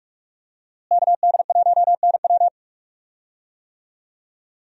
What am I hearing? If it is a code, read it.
Morse code, heard as a keyed tone, about 46 wpm, 700 Hz. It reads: KD1NW
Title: USA Call Signs - 5 Characters - Repeated 3x - 45wpm